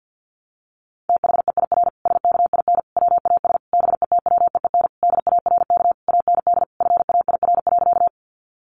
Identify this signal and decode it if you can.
Morse code, heard as a keyed tone, about 50 wpm, 700 Hz. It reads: THEIR SKIN WAS BETWEEN DARK AND FAIR.
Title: Sentences from Top 1300 Words 50wpm